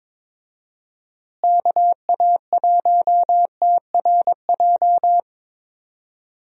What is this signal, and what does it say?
Morse code, heard as a keyed tone, about 22 wpm, 700 Hz. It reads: KA1TRJ
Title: USA Call Signs - 6 Characters - Repeated 2x - 22wpm